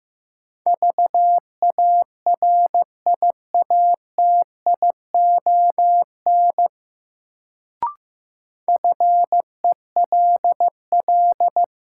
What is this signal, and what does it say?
Morse code, heard as a keyed tone, about 15 wpm, 700 Hz. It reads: VARIATION FELL
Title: Sentences from Top 1600 Words - New Word Drill - 15wpm